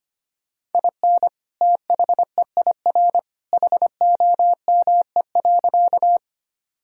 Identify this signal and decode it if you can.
Morse code, heard as a keyed tone, about 25 wpm, 700 Hz. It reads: IN THEIR HOME.
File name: practice / 25wpm / Sentences with 2-Word Phrases - 25wpm